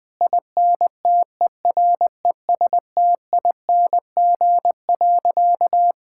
Morse code, heard as a keyed tone, about 20 wpm, 700 Hz. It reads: INTERESTING.
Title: Sentences with 2-Word Phrases - Spoken Beginning - 20wpm